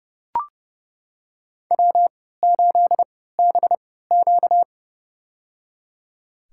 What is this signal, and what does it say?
Morse code, heard as a keyed tone, about 30 wpm, spaced as for 17 wpm, 700 Hz. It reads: W8BQ
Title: USA Call Signs - 3x Character Spacing - 30wpm